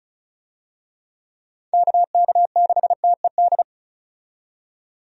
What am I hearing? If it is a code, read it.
Morse code, heard as a keyed tone, about 35 wpm, 700 Hz. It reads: KK6TED
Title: USA Call Signs - 6 Characters - 35wpm